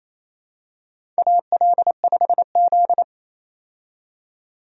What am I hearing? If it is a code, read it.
Morse code, heard as a keyed tone, about 28 wpm, 700 Hz. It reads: AL5Z
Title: USA Call Signs - 4 Characters - 28wpm